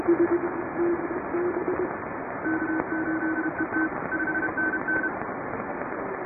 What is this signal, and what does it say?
Morse code, heard as a keyed tone, about 32 wpm, 1500 Hz, with radio noise. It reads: K8ET 5NN
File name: chatham / 2350-12CW